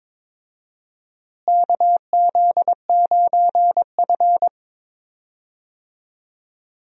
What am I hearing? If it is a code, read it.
Morse code, heard as a keyed tone, about 22 wpm, 700 Hz. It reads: KZ9F
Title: USA Call Signs - 4 Characters - Repeated 2x - 22wpm